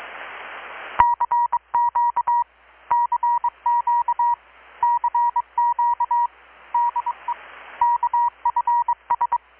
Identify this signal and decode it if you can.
Morse code, heard as a keyed tone, about 22 wpm, 950 Hz. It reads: CQ CQ CQ DE KFS
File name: blog / swl / 6-abril-2019-12695-USB-KPS